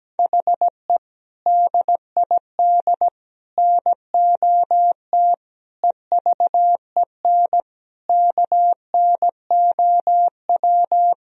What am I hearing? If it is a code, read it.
Morse code, heard as a keyed tone, about 17 wpm, 700 Hz. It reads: HE DID NOT EVEN KNOW
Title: Sentences from Top 1000 Words - Review Part 1 - 17wpm